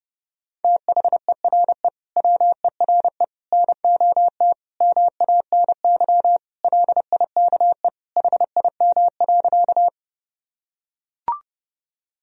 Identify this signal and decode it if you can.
Morse code, heard as a keyed tone, about 30 wpm, 700 Hz. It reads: THERE WERE NOT MANY LIKE HIM.
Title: Sentences from Top 1000 Words - Review Part 1 - 30wpm